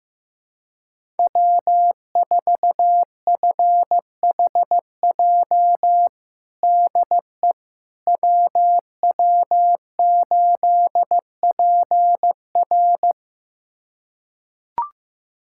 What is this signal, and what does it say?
Morse code, heard as a keyed tone, about 15 wpm, 700 Hz. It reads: W4FHJ DE WW8PR
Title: USA Callsign DE Callsign - 5 Characters - 15wpm